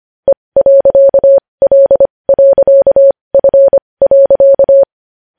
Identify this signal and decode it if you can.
Morse code, heard as a keyed tone, about 25 wpm, spaced as for 20 wpm, 550 Hz. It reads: E.L.F.
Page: news-headlines-in-morse-code-at-20-wpm